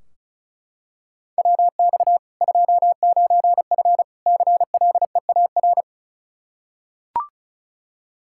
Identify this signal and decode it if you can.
Morse code, heard as a keyed tone, about 34 wpm, 700 Hz. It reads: WX 29F CLEAR
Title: Practice QSO - Exchange 2 - Info - 35wpm